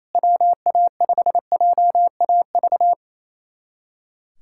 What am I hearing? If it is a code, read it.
Morse code, heard as a keyed tone, about 28 wpm, 700 Hz. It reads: WA5JAV
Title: USA Callsign DE Callsign 28wpm